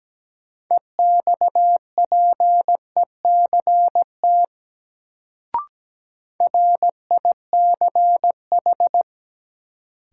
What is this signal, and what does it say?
Morse code, heard as a keyed tone, about 17 wpm, 700 Hz. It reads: EXPECT RICH
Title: Sentences from Top 1100 Words - New Word Drill - 17wpm